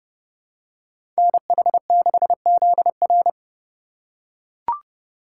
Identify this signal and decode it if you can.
Morse code, heard as a keyed tone, about 30 wpm, 700 Hz. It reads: NH6ZR
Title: USA Call Signs - 5 Characters - 2x Character Spacing - 30wpm